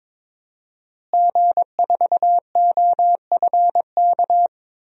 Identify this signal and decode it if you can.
Morse code, heard as a keyed tone, about 22 wpm, 700 Hz. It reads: G4OFK
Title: DXCC-100 Call Signs - 5 Characters - Repeated 3x - 22wpm